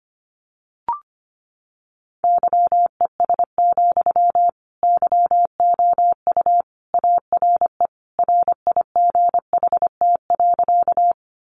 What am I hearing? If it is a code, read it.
Morse code, heard as a keyed tone, about 25 wpm, 700 Hz. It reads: YES, YOU ARE RIGHT.
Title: Sentences from Top 500 Words 25wpm